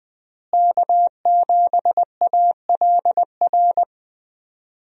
Morse code, heard as a keyed tone, about 20 wpm, 700 Hz. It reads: K7ALR
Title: USA Call Signs - 5 Characters - Repeated 2x - 20wpm